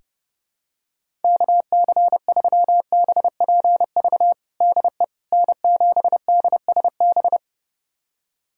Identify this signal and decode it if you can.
Morse code, heard as a keyed tone, about 30 wpm, 700 Hz. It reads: KC3BPV DE N7DSB
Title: USA Callsign DE Callsign 30wpm